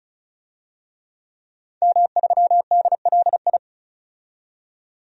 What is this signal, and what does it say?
Morse code, heard as a keyed tone, about 36 wpm, 700 Hz. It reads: M3DLI
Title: DXCC-100 Call Signs - 5 Characters - 35wpm